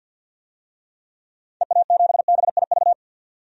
Morse code, heard as a keyed tone, about 52 wpm, 700 Hz. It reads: EA7BIV
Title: DXCC-100 Call Signs - 6 Characters - 50wpm